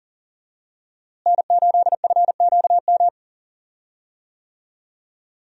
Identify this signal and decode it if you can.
Morse code, heard as a keyed tone, about 39 wpm, 700 Hz. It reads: N8FQM
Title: USA Call Signs - 5 Characters - Repeated 2x - 40wpm